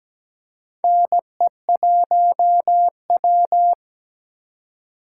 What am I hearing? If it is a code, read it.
Morse code, heard as a keyed tone, about 17 wpm, 700 Hz. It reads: NE1W
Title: USA Call Signs - 4 Characters - 2x Character Spacing - 17wpm